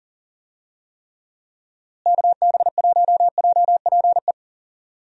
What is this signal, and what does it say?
Morse code, heard as a keyed tone, about 40 wpm, 700 Hz. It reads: KB1JPE
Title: USA Call Signs - 6 Characters - Repeated 3x - 40wpm